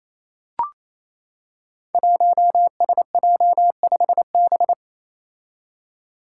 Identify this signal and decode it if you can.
Morse code, heard as a keyed tone, about 27 wpm, 700 Hz. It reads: 1SJ5B
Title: Five Challenging Letter-Number Combinations 28wpm